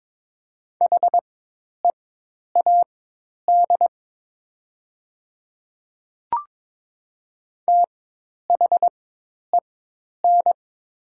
Morse code, heard as a keyed tone, about 22 wpm, spaced as for 10 wpm, 700 Hz. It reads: HEAD THEN
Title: 4x Character Spacing - Top 300 Words - 22wpm